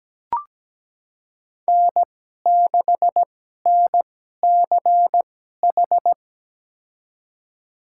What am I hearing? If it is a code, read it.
Morse code, heard as a keyed tone, about 17 wpm, spaced as for 12 wpm, 700 Hz. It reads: N6NCH
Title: USA Call Signs - 5 Characters - 2x Character Spacing - 17wpm